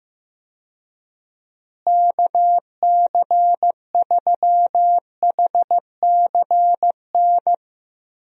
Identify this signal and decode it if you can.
Morse code, heard as a keyed tone, about 15 wpm, 700 Hz. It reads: KC3HCN
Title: USA Call Signs - 6 Characters - Repeated 2x - 15wpm